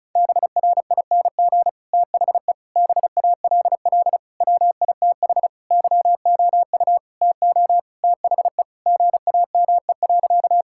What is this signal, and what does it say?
Morse code, heard as a keyed tone, about 34 wpm, 700 Hz. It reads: BRING THE BALL WITH YOU TO THE GAME.
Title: Sentences from Top 1000 Words - Review Part 6 - 35wpm